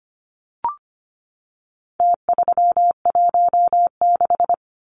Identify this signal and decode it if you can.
Morse code, heard as a keyed tone, about 25 wpm, 700 Hz. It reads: T316
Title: Numbers - Contest Style - 4-Digit 25wpm